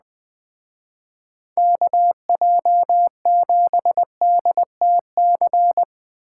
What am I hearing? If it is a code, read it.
Morse code, heard as a keyed tone, about 20 wpm, 700 Hz. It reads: KJ7DTC
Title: Speed Racing - USA 2x3 Call Signs 17wpm